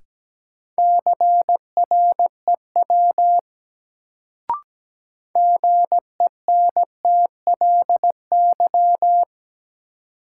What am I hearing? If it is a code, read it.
Morse code, heard as a keyed tone, about 17 wpm, 700 Hz. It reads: CREW GENTLY
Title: Sentences from Top 1500 Words - New Word Drill Alt - 17wpm